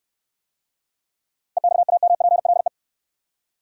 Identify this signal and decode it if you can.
Morse code, heard as a keyed tone, about 68 wpm, 700 Hz. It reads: EXAMPLE